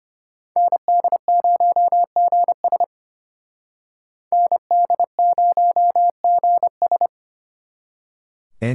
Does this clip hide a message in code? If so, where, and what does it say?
Yes: Morse code from 0.6 to 7.1 s, heard as a keyed tone, about 32 wpm, 700 Hz. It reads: ND0GS ND0GS